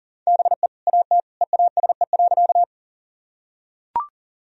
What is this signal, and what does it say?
Morse code, heard as a keyed tone, about 39 wpm, 700 Hz. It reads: BE AT EASE.
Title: Sentences from Top 1000 Words - Review Part 1 - 40wpm